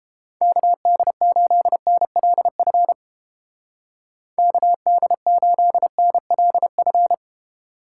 Morse code, heard as a keyed tone, about 29 wpm, 700 Hz. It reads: KD8NLF KD8NLF